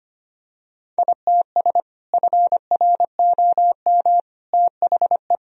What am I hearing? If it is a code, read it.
Morse code, heard as a keyed tone, about 25 wpm, 700 Hz. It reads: ITS FROM THE
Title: Sentences from Top 1000 Words - Review Part 2 - 25wpm